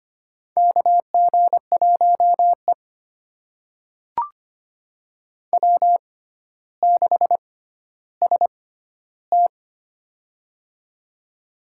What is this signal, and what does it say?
Morse code, heard as a keyed tone, about 25 wpm, spaced as for 13 wpm, 700 Hz. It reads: KG1E W6ST
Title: USA Call Signs - 6x Character Spacing - 25wpm